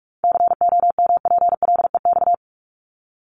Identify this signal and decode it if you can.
Morse code, heard as a keyed tone, about 44 wpm, 700 Hz. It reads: COMPLEX
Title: Top 751-1000 Words 45wpm